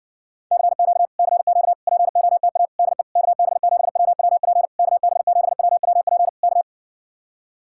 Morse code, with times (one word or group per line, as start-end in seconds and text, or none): 0.5-1.1 CQ
1.2-1.7 CQ
1.9-2.7 POTA
2.8-3.0 DE
3.2-4.7 KB7WWP
4.8-6.3 KB7WWP
6.4-6.6 K